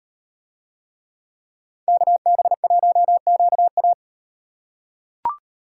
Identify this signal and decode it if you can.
Morse code, heard as a keyed tone, about 38 wpm, 700 Hz. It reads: KB1QA